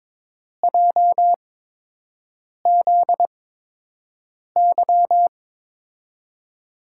Morse code, heard as a keyed tone, about 23 wpm, spaced as for 6 wpm, 700 Hz. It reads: JZY
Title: USA Call Signs - 5 Characters - 8x Character Spacing - 22wpm